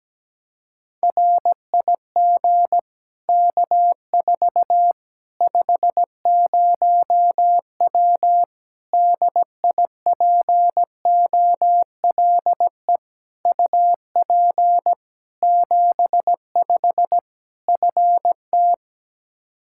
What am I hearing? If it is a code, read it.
Morse code, heard as a keyed tone, about 17 wpm, 700 Hz. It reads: RIG K4 50W DIPOLE UP 75 FT
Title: Practice QSO - Exchange 2 - Info - 17wpm